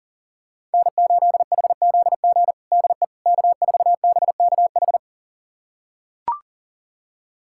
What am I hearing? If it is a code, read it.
Morse code, heard as a keyed tone, about 40 wpm, 700 Hz. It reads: N8HZG DE K4BKH
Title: USA Callsign DE Callsign 40wpm